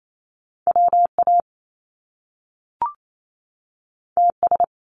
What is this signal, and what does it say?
Morse code, heard as a keyed tone, about 28 wpm, 700 Hz. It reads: WA TS
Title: Top Two Letter Combinations in Words 28wpm